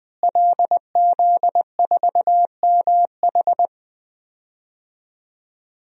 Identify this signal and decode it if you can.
Morse code, heard as a keyed tone, about 20 wpm, 700 Hz. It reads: LZ4MH
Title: DXCC-100 Call Signs - 5 Characters - Repeated 3x - 20wpm